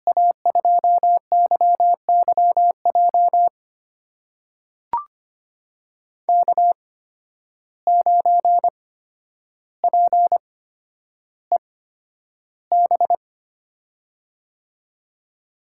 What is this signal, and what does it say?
Morse code, heard as a keyed tone, about 25 wpm, spaced as for 11 wpm, 700 Hz. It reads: A2YYJ K9PEB
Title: USA Call Signs - 8x Character Spacing - 25wpm